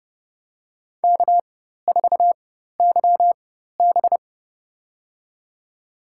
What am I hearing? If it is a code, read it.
Morse code, heard as a keyed tone, about 32 wpm, spaced as for 14 wpm, 700 Hz. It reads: K4YB
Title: USA Call Signs - 4 Characters - 4x Character Spacing - 30wpm